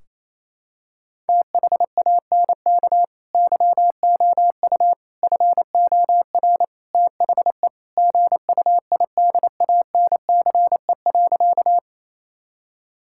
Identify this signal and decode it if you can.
Morse code, heard as a keyed tone, about 28 wpm, 700 Hz. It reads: THANK YOU FOR THE GUIDANCE.
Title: Sentences from Top 1600 Words 28wpm